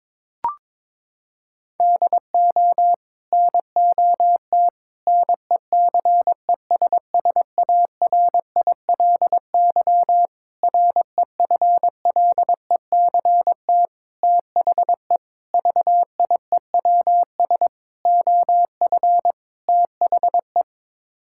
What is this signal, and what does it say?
Morse code, heard as a keyed tone, about 21 wpm, 700 Hz. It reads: DO NOT NECESSARILY REFLECT THE VIEWS OF THE
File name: practice / 22wpm / Sets of 8 Words - 22wpm